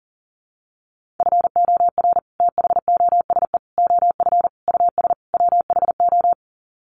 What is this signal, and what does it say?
Morse code, heard as a keyed tone, about 40 wpm, 700 Hz. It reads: FOR THOSE OF US WHO
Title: Sets of 5 Words Encore 40wpm